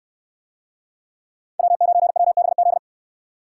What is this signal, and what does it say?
Morse code, heard as a keyed tone, about 67 wpm, 700 Hz. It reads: W9WBZ